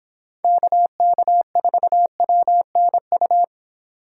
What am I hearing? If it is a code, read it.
Morse code, heard as a keyed tone, about 27 wpm, 700 Hz. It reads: KK4WNU
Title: Speed Racing - USA 2x3 Call Signs 22wpm